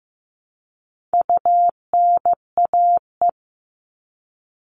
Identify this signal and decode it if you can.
Morse code, heard as a keyed tone, about 15 wpm, 700 Hz. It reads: UNAE
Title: All Cut Numbers - 4-Digit 15wpm